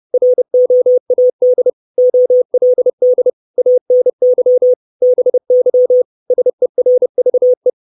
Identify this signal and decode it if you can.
Morse code, heard as a keyed tone, about 30 wpm, 500 Hz. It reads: ROAD OLD ANY BY SERVE
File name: 632 Words 30WPM